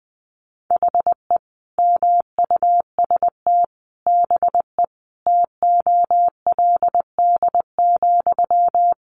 Morse code, heard as a keyed tone, about 20 wpm, 700 Hz. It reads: HE MUST BE TOLD,
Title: Sentences from Top 1400 Words 20wpm